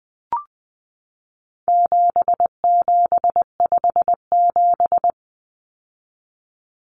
Morse code, heard as a keyed tone, about 20 wpm, 700 Hz. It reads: 7757
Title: Numbers - 4-Digit 20wpm